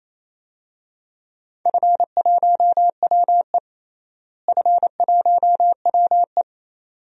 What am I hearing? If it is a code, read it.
Morse code, heard as a keyed tone, about 28 wpm, 700 Hz. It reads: F1WE F1WE
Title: DXCC-100 Call Signs - 4 Characters - Repeated 2x - 28wpm